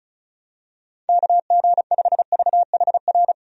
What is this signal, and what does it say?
Morse code, heard as a keyed tone, about 34 wpm, 700 Hz. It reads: KG5VHR
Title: USA Call Signs - 6 Characters - 35wpm